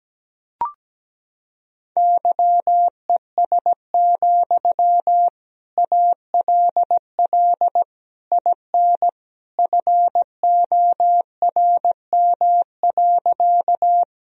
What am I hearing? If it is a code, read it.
Morse code, heard as a keyed tone, about 17 wpm, 700 Hz. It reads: YES, ALL IN FORM.
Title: Sentences from Top 1000 Words - Review Part 1 - 17wpm